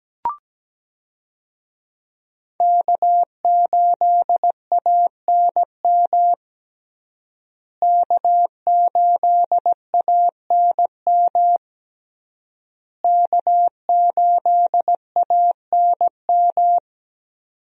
Morse code, heard as a keyed tone, about 17 wpm, 700 Hz. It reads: K8ANM K8ANM K8ANM
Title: USA Call Signs - 5 Characters - Repeated 3x - 17wpm